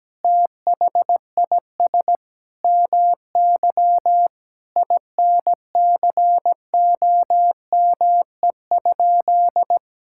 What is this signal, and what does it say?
Morse code, heard as a keyed tone, about 17 wpm, 700 Hz. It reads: THIS MY INCOME?
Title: Sentences from Top 1000 Words - Review Part 5 - 17wpm